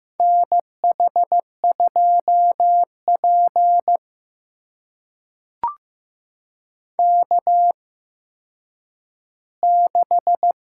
Morse code, heard as a keyed tone, about 15 wpm, spaced as for 9 wpm, 700 Hz. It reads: NH2P K6
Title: USA Call Signs - 4 Characters - 8x Character Spacing - 15wpm